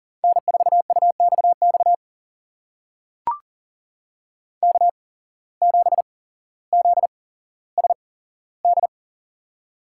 Morse code, heard as a keyed tone, about 40 wpm, spaced as for 17 wpm, 700 Hz. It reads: N4UXX K7ZSD
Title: USA Call Signs - 5 Characters - 8x Character Spacing - 40wpm